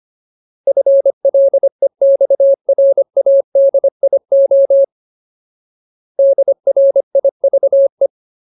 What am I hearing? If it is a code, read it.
Morse code, heard as a keyed tone, about 25 wpm, 550 Hz. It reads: FLEXRADIO DRIVE
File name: Ultimate Word List / 25WPM Ultimate